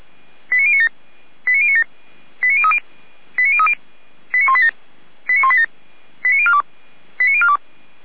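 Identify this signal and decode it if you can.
Five-tone selective call (ZVEI), heard as quick runs of five tones, tones of about 70 ms. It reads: 89097 89097 89030 89030 89187 89187 89042 89042
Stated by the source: ZVEI five-tone selective call received on an SDR.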